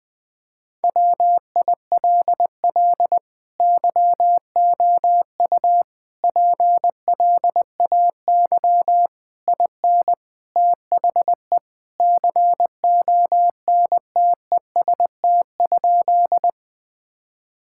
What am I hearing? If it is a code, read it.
Morse code, heard as a keyed tone, about 21 wpm, 700 Hz. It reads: WILL YOU PLAY IN THE CONTEST?
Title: Sentences from Top 1600 Words 20wpm